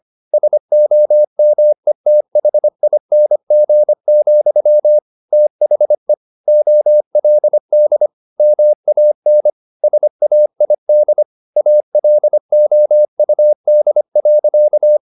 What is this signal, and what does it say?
Morse code, heard as a keyed tone, about 25 wpm, 600 Hz. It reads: SOMETHING, THE OLD MAN SAID ALOUD.
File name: the old man and the sea part 1 25wpm